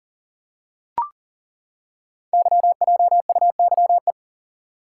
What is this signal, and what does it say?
Morse code, heard as a keyed tone, about 41 wpm, 700 Hz. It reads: YJUYE
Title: Code Groups - 5 Letters - 40wpm